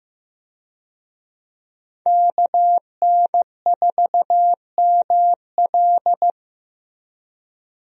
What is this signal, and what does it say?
Morse code, heard as a keyed tone, about 16 wpm, 700 Hz. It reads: KN4ML
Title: USA Call Signs - 5 Characters - Repeated 2x - 15wpm